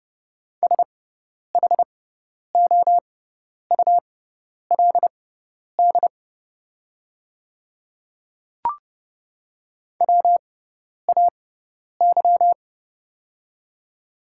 Morse code, heard as a keyed tone, about 30 wpm, spaced as for 10 wpm, 690 Hz. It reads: SHOULD WAY
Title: 6x Character Spacing - Top 500 Words - 30wpm